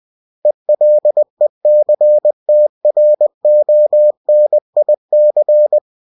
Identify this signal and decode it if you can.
Morse code, heard as a keyed tone, about 20 wpm, 600 Hz. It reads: ELECTRONIC
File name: cw-books / winnie-the-pooh / winnie-20wpm-page-0113